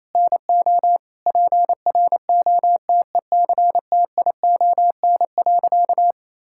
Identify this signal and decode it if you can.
Morse code, heard as a keyed tone, about 28 wpm, 700 Hz. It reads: NO PROTECTION.
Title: Sentences from Top 1600 Words 28wpm